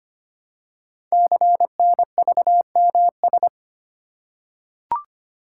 Morse code, heard as a keyed tone, about 25 wpm, 700 Hz. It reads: CNVMS
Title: Code Groups - 5 Letters - 25wpm